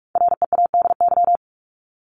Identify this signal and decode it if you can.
Morse code, heard as a keyed tone, about 44 wpm, 700 Hz. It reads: READY
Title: LY Suffix - Top 100 Words with Base Words 45wpm